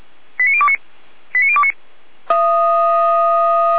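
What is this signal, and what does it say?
Five-tone selective call (ZVEI), heard as quick runs of five tones, tones of about 70 ms. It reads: 89029 89029
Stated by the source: ZVEI five-tone call received on SDR, logged by Feuerwehr Orlamünde.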